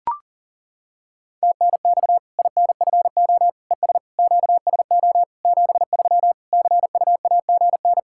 Morse code, heard as a keyed tone, about 40 wpm, 700 Hz. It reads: TNX INFO ES QSO 73 CUAGN
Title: Practice QSO - Exchange 4 - Ending - 40wpm